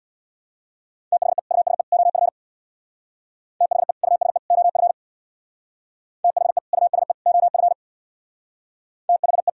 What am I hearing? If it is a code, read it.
Morse code, heard as a keyed tone, about 64 wpm, 700 Hz. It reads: THE USE OF THE USE OF THE USE OF THE